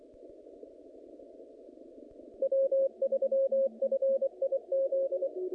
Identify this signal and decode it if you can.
Morse code, heard as a keyed tone, about 23 wpm, 540 Hz. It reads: W3FIZ
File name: contests / naqp-cw-k5tr-op-k5pi- -2300